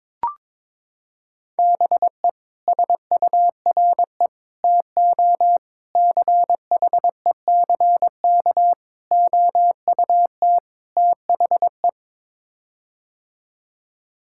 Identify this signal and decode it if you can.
Morse code, heard as a keyed tone, about 22 wpm, 700 Hz. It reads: BE SURE TO CHECK OUT THE
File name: practice / 22wpm / Sets of 6 Words - 22wpm